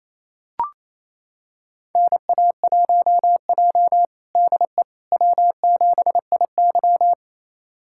Morse code, heard as a keyed tone, about 28 wpm, 700 Hz. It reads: NA1J DE W7IY
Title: USA Callsign DE Callsign - 4 Characters - 28wpm